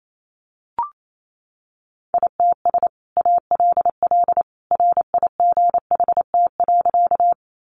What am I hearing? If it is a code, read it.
Morse code, heard as a keyed tone, about 28 wpm, 700 Hz. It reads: ITS ALL RIGHT.
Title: Sentences from Top 600 Words 28wpm